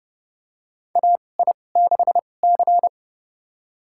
Morse code, heard as a keyed tone, about 30 wpm, spaced as for 22 wpm, 700 Hz. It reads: AI6C
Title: USA Call Signs - 4 Characters - 2x Character Spacing - 30wpm